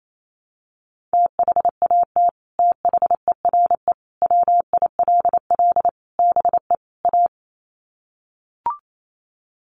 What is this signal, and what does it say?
Morse code, heard as a keyed tone, about 28 wpm, 700 Hz. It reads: THAT THERE WILL BE A
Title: Sets of 5 Words Encore2 28wpm